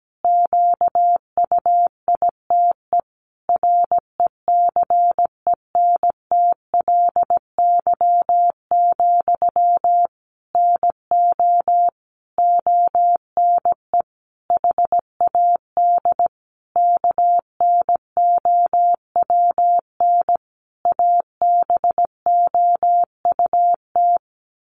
Morse code, heard as a keyed tone, about 17 wpm, 700 Hz. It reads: QUITE RECENTLY, NO ONE HAD KNOWN ABOUT
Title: Sentences from Top 1200 Words 17wpm